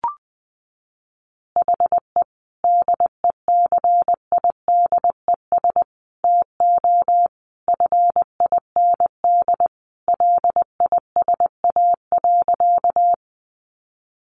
Morse code, heard as a keyed tone, about 19 wpm, 700 Hz. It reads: HE DECIDES TO FIND LISA.
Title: Additional Easy Phrases 20wpm